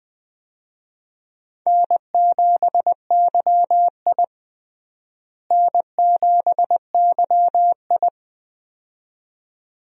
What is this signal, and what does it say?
Morse code, heard as a keyed tone, about 20 wpm, 700 Hz. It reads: N7YI N7YI
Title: USA Call Signs - 4 Characters - Repeated 2x - 20wpm